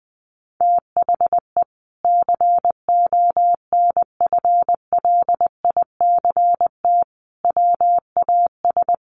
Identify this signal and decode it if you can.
Morse code, heard as a keyed tone, about 20 wpm, 700 Hz. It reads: THE CONFLICT WAS
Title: Sentences from Top 1300 Words 20wpm